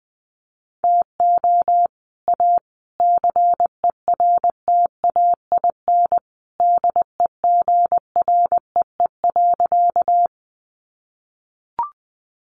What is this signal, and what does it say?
Morse code, heard as a keyed tone, about 20 wpm, 700 Hz. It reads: TO A CERTAIN DEGREE.
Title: Sentences from Top 1500 Words 20wpm